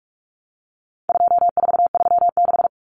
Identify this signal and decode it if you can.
Morse code, heard as a keyed tone, about 44 wpm, 700 Hz. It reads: 2436
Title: Numbers - 4-Digit 45wpm